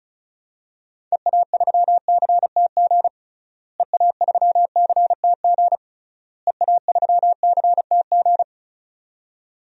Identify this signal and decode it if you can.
Morse code, heard as a keyed tone, about 36 wpm, 700 Hz. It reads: EA3CTG EA3CTG EA3CTG
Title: DXCC-100 Call Signs - 6 Characters - Repeated 3x - 35wpm